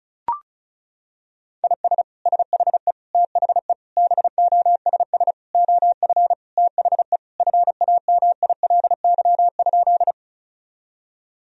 Morse code, heard as a keyed tone, about 36 wpm, 700 Hz. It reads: IS SHE THE BOSS OF THE FAMILY?
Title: Sentences from Top 1000 Words - Review Part 4 - 35wpm